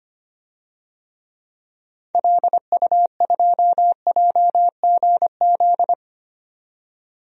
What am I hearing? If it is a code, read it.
Morse code, heard as a keyed tone, about 25 wpm, 700 Hz. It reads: LU2JGZ
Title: DXCC-100 Call Signs - 6 Characters - 25wpm